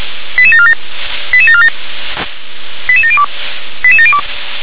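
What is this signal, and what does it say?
Five-tone selective call (ZVEI), heard as quick runs of five tones, tones of about 80 ms. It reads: 88747 88747 88702 88702
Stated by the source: ZVEI five-tone call 88915 received on an SDR.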